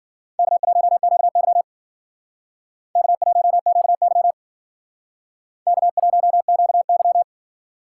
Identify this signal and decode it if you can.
Morse code, heard as a keyed tone, about 54 wpm, 700 Hz. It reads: K1QY K1QY K1QY